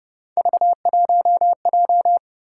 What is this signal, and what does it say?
Morse code, heard as a keyed tone, about 27 wpm, 700 Hz. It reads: V1J